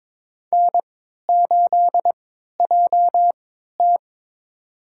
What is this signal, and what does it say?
Morse code, heard as a keyed tone, about 22 wpm, spaced as for 13 wpm, 700 Hz. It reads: N8JT
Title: USA Call Signs - 4 Characters - 3x Character Spacing - 22wpm